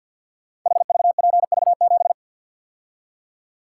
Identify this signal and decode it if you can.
Morse code, heard as a keyed tone, about 50 wpm, 700 Hz. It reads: HVPV7
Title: Code Groups - 5 Letter:Numbers - 50wpm